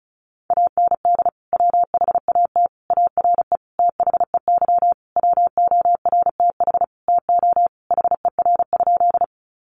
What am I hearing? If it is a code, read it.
Morse code, heard as a keyed tone, about 34 wpm, 700 Hz. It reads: AND WHAT ARE THEY WORTH TO HER?
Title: Sentences from Top 800 Words 35wpm